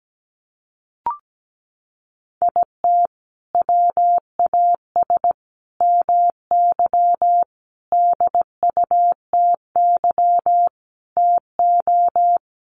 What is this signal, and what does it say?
Morse code, heard as a keyed tone, about 17 wpm, 700 Hz. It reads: IT WAS MY DUTY TO
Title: Sentences from Top 1500 Words 17wpm